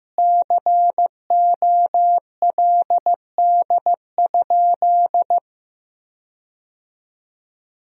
Morse code, heard as a keyed tone, about 15 wpm, 700 Hz. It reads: COLD?
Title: 4x Word-Spacing - Sentences from Top 400 Words - 15wpm